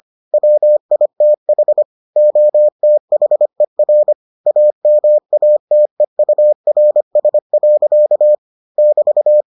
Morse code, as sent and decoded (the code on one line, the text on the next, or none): .-- .. - .... / --- - .... .-. / .- -- .- - . ..- .-. ... .-.-.- / -...-
WITH OTHER AMATEURS. <BT>